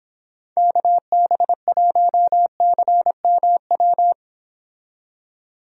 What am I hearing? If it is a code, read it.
Morse code, heard as a keyed tone, about 25 wpm, 700 Hz. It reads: KB1CMW